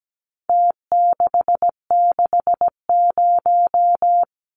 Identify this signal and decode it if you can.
Morse code, heard as a keyed tone, about 17 wpm, 700 Hz. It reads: T660
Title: Numbers - Contest Style - 4-Digit 17wpm